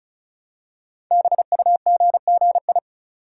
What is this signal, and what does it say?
Morse code, heard as a keyed tone, about 36 wpm, 690 Hz. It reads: BUGGI
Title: Code Groups - 5 Letter:Numbers - 35wpm